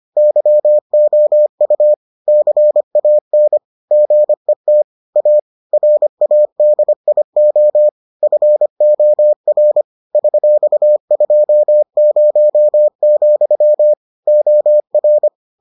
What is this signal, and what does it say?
Morse code, heard as a keyed tone, about 25 wpm, 600 Hz. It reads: YOU CAN GET A RADIO FOR $20, OR